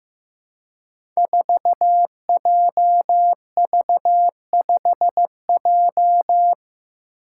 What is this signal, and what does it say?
Morse code, heard as a keyed tone, about 15 wpm, 700 Hz. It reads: 4JV5J
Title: Five Challenging Letter-Number Combinations 15wpm